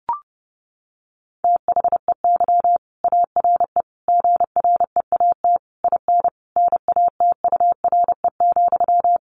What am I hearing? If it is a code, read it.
Morse code, heard as a keyed tone, about 30 wpm, 700 Hz. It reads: THEY ARE GREAT IN NATURE,
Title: Sentences from Top 1400 Words 30wpm